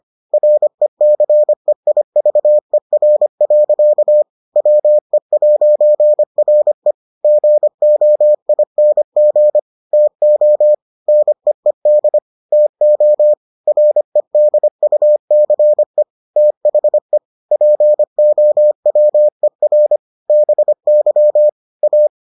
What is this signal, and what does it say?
Morse code, heard as a keyed tone, about 25 wpm, 600 Hz. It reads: RECEIVER. WE'RE GOING TO NEED TO REDUCE THE POWER BY A